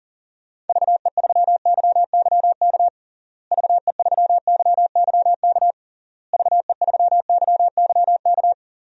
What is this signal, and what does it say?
Morse code, heard as a keyed tone, about 40 wpm, 700 Hz. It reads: VE3YYK VE3YYK VE3YYK
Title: DXCC-100 Call Signs - 6 Characters - Repeated 3x - 40wpm